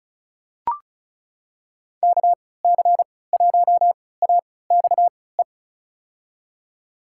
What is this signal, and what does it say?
Morse code, heard as a keyed tone, about 36 wpm, spaced as for 20 wpm, 700 Hz. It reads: KC1AXE
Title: USA Call Signs - 6 Characters - 3x Character Spacing - 35wpm